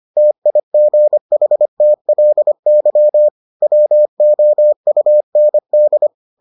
Morse code, heard as a keyed tone, about 25 wpm, 600 Hz. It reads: TIGHTLY WOUND